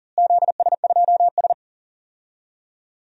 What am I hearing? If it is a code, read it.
Morse code, heard as a keyed tone, about 40 wpm, 700 Hz. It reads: ZS2S
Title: DXCC-100 Call Signs - 4 Characters - Repeated 2x - 40wpm